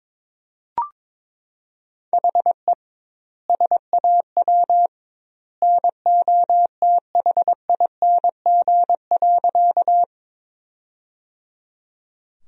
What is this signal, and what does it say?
Morse code, heard as a keyed tone, about 22 wpm, 700 Hz. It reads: HE SAW NOTHING.
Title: 2x Word-Spacing - Sentences from Top 500 Words - 22wpm